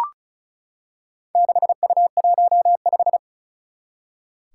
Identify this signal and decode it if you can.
Morse code, heard as a keyed tone, about 36 wpm, 700 Hz. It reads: 6U15
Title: Four Challenging Letter-Number Combinations 35wpm